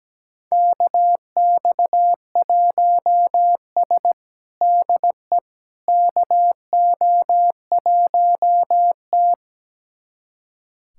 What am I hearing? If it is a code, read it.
Morse code, heard as a keyed tone, about 17 wpm, 700 Hz. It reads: KX1S DE KO1T
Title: USA Callsign DE Callsign - 4 Characters - 17wpm